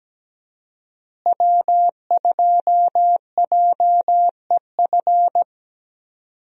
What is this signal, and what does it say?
Morse code, heard as a keyed tone, about 17 wpm, 700 Hz. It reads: W2JEF
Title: USA Call Signs - 5 Characters - Repeated 2x - 17wpm